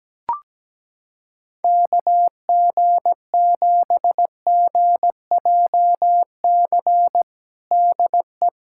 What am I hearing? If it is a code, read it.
Morse code, heard as a keyed tone, about 17 wpm, 700 Hz. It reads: KG7GJC DE
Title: USA Callsign DE Callsign 17wpm